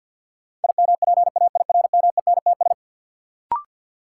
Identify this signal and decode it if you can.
Morse code, heard as a keyed tone, about 52 wpm, 700 Hz. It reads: IMPAIRMENTS
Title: Sets of 5 Words Encore - New Word Drill - 50wpm